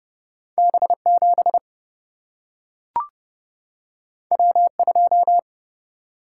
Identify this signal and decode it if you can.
Morse code, heard as a keyed tone, about 30 wpm, 700 Hz. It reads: B 7 W 2
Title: Single-Letter Single-Number 30wpm